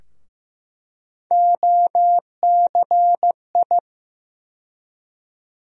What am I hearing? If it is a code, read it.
Morse code, heard as a keyed tone, about 15 wpm, 700 Hz. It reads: OCI
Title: USA Call Sign Postfixes - 3 Letters 15wpm